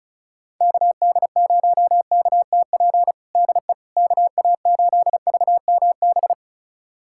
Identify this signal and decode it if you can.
Morse code, heard as a keyed tone, about 34 wpm, 700 Hz. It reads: KD0KTP DE KA8VMB
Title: USA Callsign DE Callsign 35wpm